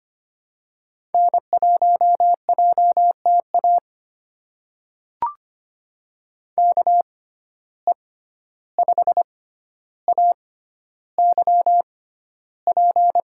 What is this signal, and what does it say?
Morse code, heard as a keyed tone, about 25 wpm, spaced as for 12 wpm, 700 Hz. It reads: N1JTA KE5AYP
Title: USA Call Signs - 6x Character Spacing - 25wpm